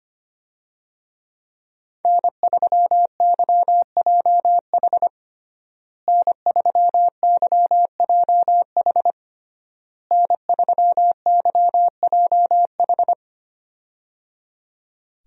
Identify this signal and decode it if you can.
Morse code, heard as a keyed tone, about 25 wpm, 700 Hz. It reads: N3YJH N3YJH N3YJH
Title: USA Call Signs - 5 Characters - Repeated 3x - 25wpm